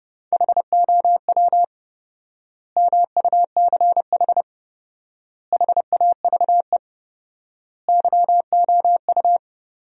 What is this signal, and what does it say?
Morse code, heard as a keyed tone, about 30 wpm, 700 Hz. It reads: HOW MUCH HAVE YOU
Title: 4x Word-Spacing - Sentences from Top 300 Words - 30wpm